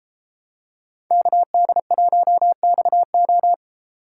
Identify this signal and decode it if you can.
Morse code, heard as a keyed tone, about 32 wpm, 700 Hz. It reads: KD1XO